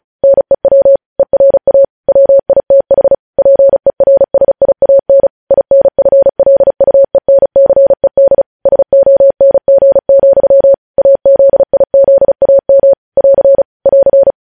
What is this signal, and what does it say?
Morse code, heard as a keyed tone, about 36 wpm, 550 Hz. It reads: NEW ERA WITH PERSIAN INFLUENCED SONG, AZIZAM <AR> <AR>
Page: news headlines in morse code at 35 wpm